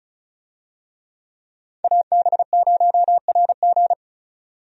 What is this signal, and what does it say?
Morse code, heard as a keyed tone, about 34 wpm, 700 Hz. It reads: AB0RG
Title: USA Call Signs - 5 Characters - 35wpm